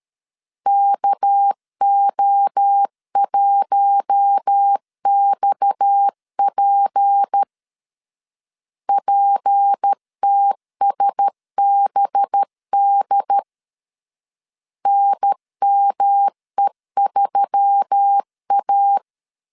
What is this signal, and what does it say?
Morse code, heard as a keyed tone, about 13 wpm, 800 Hz. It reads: KO1XP PTSBD NME3A
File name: LEZIONI / 6 PKX0 / 6 PKX 12wpm 2024